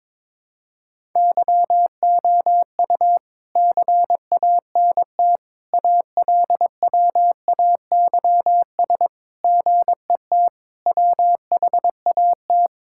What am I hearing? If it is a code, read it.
Morse code, heard as a keyed tone, about 22 wpm, 700 Hz. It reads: YOU CANT ALWAYS GET WHAT
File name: practice / 22wpm / Sets of 7 Words - 22wpm